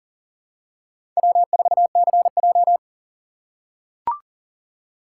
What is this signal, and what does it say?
Morse code, heard as a keyed tone, about 40 wpm, 700 Hz. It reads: W4CJ
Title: USA Call Signs - 4 Characters - 4x Character Spacing - 40wpm